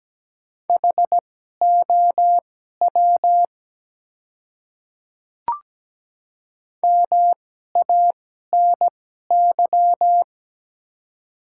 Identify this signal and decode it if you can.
Morse code, heard as a keyed tone, about 17 wpm, spaced as for 12 wpm, 700 Hz. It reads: HOW MANY
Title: 2x Character Spacing - Top 200 Words - 17wpm